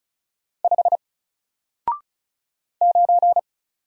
Morse code, heard as a keyed tone, about 35 wpm, 700 Hz. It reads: 5 9
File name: Single-Letter-Number-35wpm